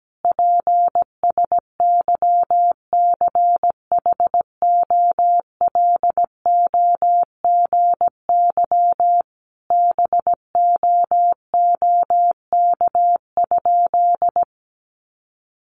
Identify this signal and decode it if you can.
Morse code, heard as a keyed tone, about 17 wpm, 700 Hz. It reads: PSYCHOLOGY BOOK?
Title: Sentences from Top 1400 Words 17wpm